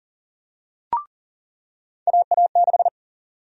Morse code, heard as a keyed tone, about 40 wpm, 700 Hz. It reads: AA6
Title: USA Call Sign Prefixes - 40wpm